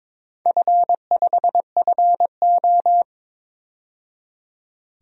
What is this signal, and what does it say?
Morse code, heard as a keyed tone, about 22 wpm, 700 Hz. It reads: F5FO
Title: DXCC-100 Call Signs - 4 Characters - 22wpm